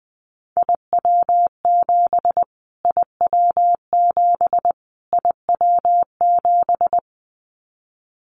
Morse code, heard as a keyed tone, about 19 wpm, 700 Hz. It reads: IW7 IW7 IW7
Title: Callsign Prefixes for Top 25 DXCC - Repeated 3x - 20wpm